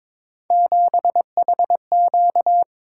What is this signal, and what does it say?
Morse code, heard as a keyed tone, about 22 wpm, 700 Hz. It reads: 7HQ
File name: practice / 22wpm / Three Letter-Number Combinations - 22wpm